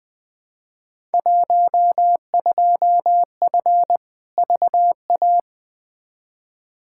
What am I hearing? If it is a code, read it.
Morse code, heard as a keyed tone, about 20 wpm, 700 Hz. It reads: 12F VA
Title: Field Day Exchanges Encore - 20wpm